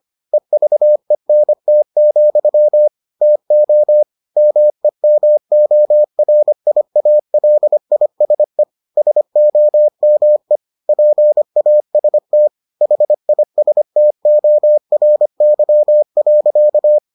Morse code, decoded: EVENT, TO MEMORIALISE SOME PAST HISTORY.

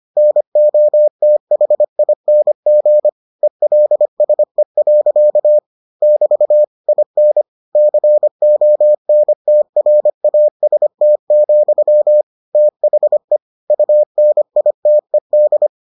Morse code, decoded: NOTHING ELSE. <BT> IN CONTRAST, THE UNITED

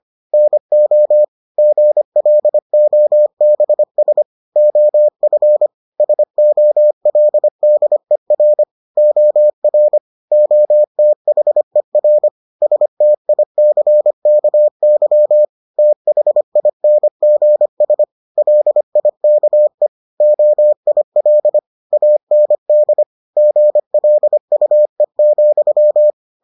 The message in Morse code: NO GLOBS OF SOLDER OR OTHER STICKY THINGS LIKE OIL AND GLUE,

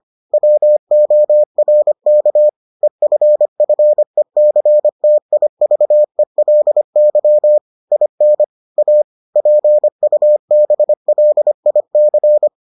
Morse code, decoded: WORK EFFECTIVELY IN A PUBLIC

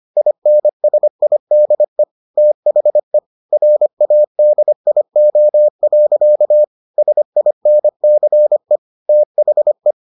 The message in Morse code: INSIDE THE RADIO. SINCE THE

